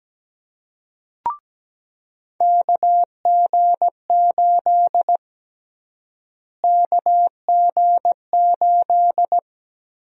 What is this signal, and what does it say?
Morse code, heard as a keyed tone, about 17 wpm, 700 Hz. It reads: KG8 KG8
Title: USA Call Sign Prefixes - Repeated 2x - 17wpm